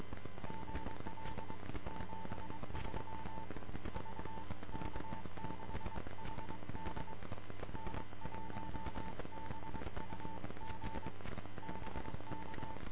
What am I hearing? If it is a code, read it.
Morse code, heard as a keyed tone, about 23 wpm, 890 Hz, with noise. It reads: GL 73 VK6HD N1BUG BK
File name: vk6hd- -x